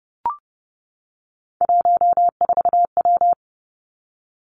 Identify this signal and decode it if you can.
Morse code, heard as a keyed tone, about 30 wpm, 700 Hz. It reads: 14W
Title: Three Challenging Letter-Number Combinations 30wpm